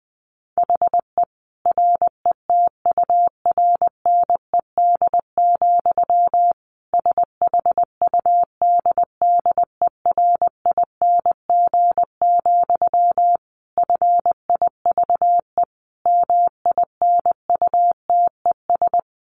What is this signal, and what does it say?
Morse code, heard as a keyed tone, about 20 wpm, 700 Hz. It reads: HE RETURNED, SHUDDERING, FIVE MINUTES
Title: A Sound Of Thunder 20wpm